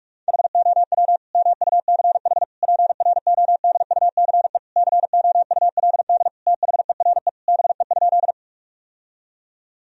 Morse code, heard as a keyed tone, about 41 wpm, 700 Hz. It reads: HOW MUCH PRODUCE COULD THERE BE?
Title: Sentences from Top 1000 Words - Review Part 3 - 45wpm